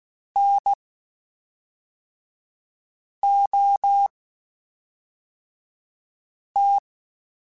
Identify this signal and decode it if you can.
Morse code, heard as a keyed tone, about 16 wpm, spaced as for 3 wpm, 800 Hz. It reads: NOT
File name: Quotes Spaced / Quotes Spaced-mp3s / IgnoringFacts sp @5